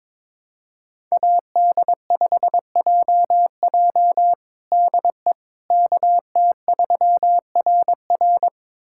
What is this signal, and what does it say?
Morse code, heard as a keyed tone, about 22 wpm, 700 Hz. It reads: AD5JJ DE KT3RR
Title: USA Callsign DE Callsign - 5 Characters - 22wpm